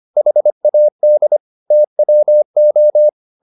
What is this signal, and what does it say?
Morse code, heard as a keyed tone, about 25 wpm, 600 Hz. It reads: HAD TWO